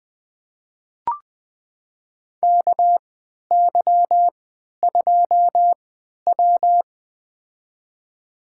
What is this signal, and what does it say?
Morse code, heard as a keyed tone, about 20 wpm, spaced as for 11 wpm, 700 Hz. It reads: KY2W
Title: USA Call Signs - 4 Characters - 3x Character Spacing - 20wpm